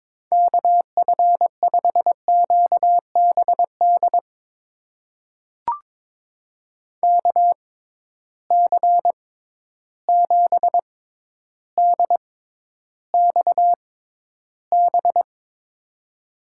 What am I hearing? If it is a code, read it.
Morse code, heard as a keyed tone, about 22 wpm, spaced as for 11 wpm, 700 Hz. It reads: KF5QBD KC7DXB
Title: USA Call Signs - 6 Characters - 6x Character Spacing - 22wpm